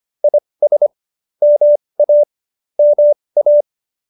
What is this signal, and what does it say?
Morse code, heard as a keyed tone, about 25 wpm, spaced as for 20 wpm, 590 Hz. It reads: IS MA MA